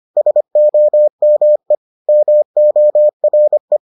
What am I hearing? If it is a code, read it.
Morse code, heard as a keyed tone, about 25 wpm, 600 Hz. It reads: SOME MORE